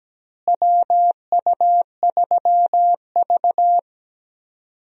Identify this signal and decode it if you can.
Morse code, heard as a keyed tone, about 17 wpm, 700 Hz. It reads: WU3V
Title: USA Call Signs - 4 Characters - Repeated 3x - 17wpm